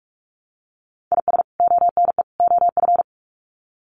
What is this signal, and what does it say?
Morse code, heard as a keyed tone, about 44 wpm, 700 Hz. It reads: IS ONE OF
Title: Sets of 3 Words 45wpm